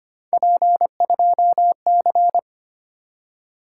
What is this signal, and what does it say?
Morse code, heard as a keyed tone, about 25 wpm, 700 Hz. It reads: P2C
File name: Three Letter-Number Combinations - 25wpm